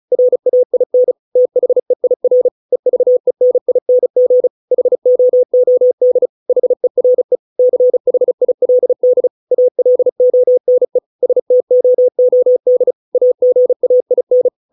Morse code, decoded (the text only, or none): RAIN THEIR EVENING HOOD HERE CHILD ALONE STOOD AGAIN